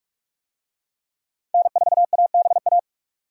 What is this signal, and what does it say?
Morse code, heard as a keyed tone, about 46 wpm, 700 Hz. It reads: N4ABA
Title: USA Call Signs - 5 Characters - 45wpm